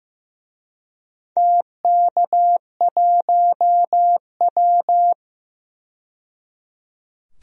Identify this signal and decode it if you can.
Morse code, heard as a keyed tone, about 15 wpm, 700 Hz. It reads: TK1W
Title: DXCC-100 Call Signs - 4 Characters - 15wpm